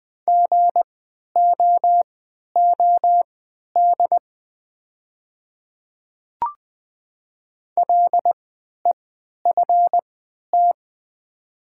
Morse code, heard as a keyed tone, about 20 wpm, spaced as for 11 wpm, 700 Hz. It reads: GOOD LEFT
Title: 3x Character Spacing - Top 300 Words - 20wpm